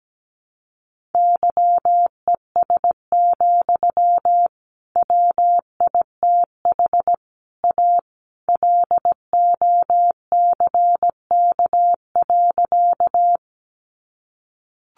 Morse code, heard as a keyed tone, about 17 wpm, 700 Hz. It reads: YES, WITH A LOCK.
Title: Sentences from Top 1300 Words 17wpm